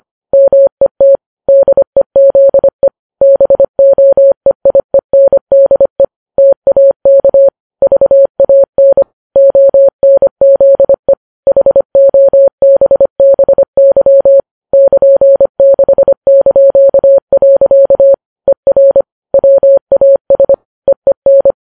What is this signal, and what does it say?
Morse code, heard as a keyed tone, about 25 wpm, 550 Hz. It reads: MET DEZE BOEIENDE TAK VAN ONZE HOBBY <KN>6). ER WAS EEN
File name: Files Web tls / Morse on9fv / morse25kps